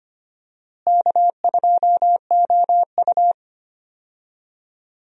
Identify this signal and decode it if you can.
Morse code, heard as a keyed tone, about 25 wpm, 700 Hz. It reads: K2OU
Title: USA Call Signs - 4 Characters - Repeated 3x - 25wpm